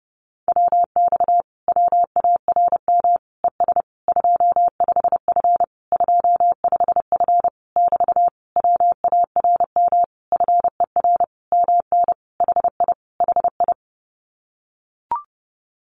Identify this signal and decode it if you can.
Morse code, heard as a keyed tone, about 30 wpm, 700 Hz. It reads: WX WARM ES 25F 25F <BT> WARM FER MN HI HI
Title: Common QSO Phrases Encore 30wpm